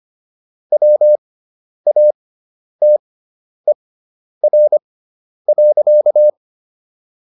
Morse code, heard as a keyed tone, about 25 wpm, spaced as for 10 wpm, 600 Hz. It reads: WATER.